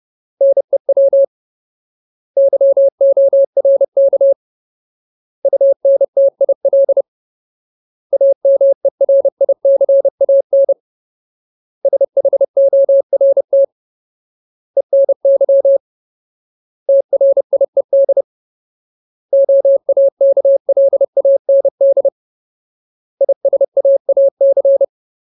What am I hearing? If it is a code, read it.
Morse code, heard as a keyed tone, about 30 wpm, 550 Hz. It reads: NEW YORK UNTIL AMERICAN SHORT ENY TRIED OAKLAND ISAAC
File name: Ultimate Word List / 30WPM Ultimate